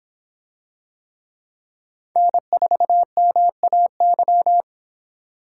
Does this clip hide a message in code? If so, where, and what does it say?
Yes: Morse code from 2.2 to 4.6 s, heard as a keyed tone, about 23 wpm, 700 Hz. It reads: N4MAY